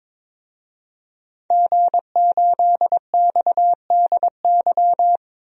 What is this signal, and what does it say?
Morse code, heard as a keyed tone, about 22 wpm, 700 Hz. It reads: G8XDY
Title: DXCC-100 Call Signs - 5 Characters - Repeated 3x - 22wpm